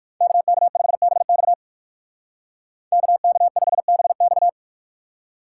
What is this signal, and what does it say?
Morse code, heard as a keyed tone, about 55 wpm, 700 Hz. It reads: KK5BX KK5BX
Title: Speed Racing - USA Call Signs 45wpm